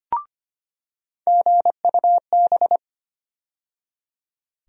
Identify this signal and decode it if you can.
Morse code, heard as a keyed tone, about 25 wpm, 700 Hz. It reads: GUB
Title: USA Call Sign Postfixes - 3 Letters 25wpm